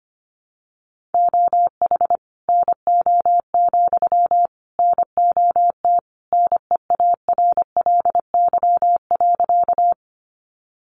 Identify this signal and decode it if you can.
Morse code, heard as a keyed tone, about 25 wpm, 700 Hz. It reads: OH NO, NOT NEARLY.
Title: Sentences from Top 1300 Words 25wpm